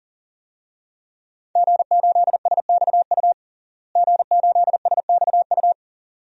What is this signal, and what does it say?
Morse code, heard as a keyed tone, about 40 wpm, 700 Hz. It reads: G8SXU G8SXU
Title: DXCC-100 Call Signs - 5 Characters - Repeated 2x - 40wpm